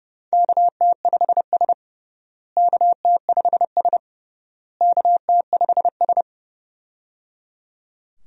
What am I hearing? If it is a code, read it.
Morse code, heard as a keyed tone, about 29 wpm, 700 Hz. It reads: KT5S KT5S KT5S
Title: USA Call Signs - 4 Characters - Repeated 3x - 30wpm